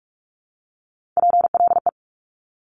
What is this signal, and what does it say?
Morse code, heard as a keyed tone, about 46 wpm, 700 Hz. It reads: PLE
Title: Top Three Letter Combinations in Words 45wpm